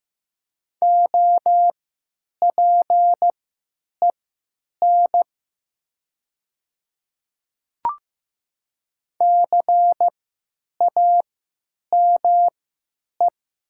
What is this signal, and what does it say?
Morse code, heard as a keyed tone, about 15 wpm, spaced as for 9 wpm, 700 Hz. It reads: OPEN CAME
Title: 3x Character Spacing - Top 300 Words - 15wpm